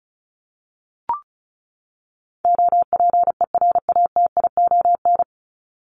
Morse code, heard as a keyed tone, about 34 wpm, 700 Hz. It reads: OPERATION
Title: Top 1501-1750 Words 35wpm